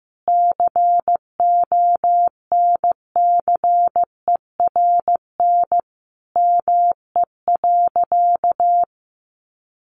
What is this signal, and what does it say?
Morse code, heard as a keyed tone, about 15 wpm, 700 Hz. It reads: CONCERN ME.
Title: Sentences from Top 1000 Words 15wpm